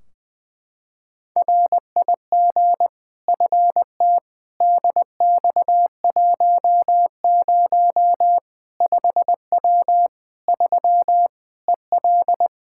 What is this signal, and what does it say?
Morse code, heard as a keyed tone, about 21 wpm, 700 Hz. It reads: RIG FT DX10 5W 3 EL
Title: Practice QSO - Exchange 2 - Info - 20wpm